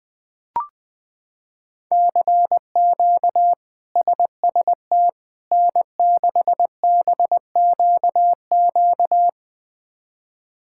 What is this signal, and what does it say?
Morse code, heard as a keyed tone, about 20 wpm, 700 Hz. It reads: CQ SST N6BQQ
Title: K1USN SST QSO Practice - 20wpm